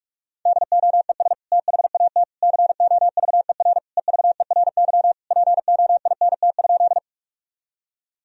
Morse code, heard as a keyed tone, about 41 wpm, 700 Hz. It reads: DOES THAT COVER EVERY POINT?
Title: Sentences from Top 1000 Words - Review Part 1 - 45wpm